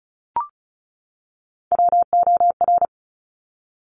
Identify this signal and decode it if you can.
Morse code, heard as a keyed tone, about 36 wpm, 700 Hz. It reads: WOR
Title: Top Three Letter Combinations in Words 35wpm